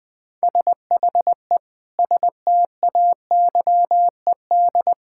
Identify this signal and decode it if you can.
Morse code, heard as a keyed tone, about 21 wpm, 700 Hz. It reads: SHE STAYED
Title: Sentences with 5-Word Phrases - 20wpm